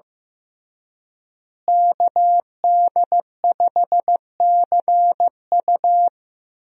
Morse code, heard as a keyed tone, about 15 wpm, 700 Hz. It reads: KD5CU
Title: USA Call Signs - 5 Characters - Repeated 3x - 15wpm